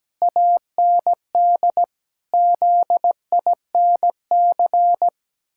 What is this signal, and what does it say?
Morse code, heard as a keyed tone, about 17 wpm, 700 Hz. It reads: AND ZINC
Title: Pangram Challenge v2 - 17wpm